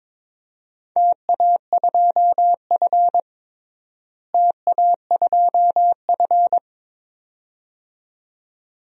Morse code, heard as a keyed tone, about 22 wpm, 700 Hz. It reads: TA2F TA2F
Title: DXCC-100 Call Signs - 4 Characters - Repeated 2x - 22wpm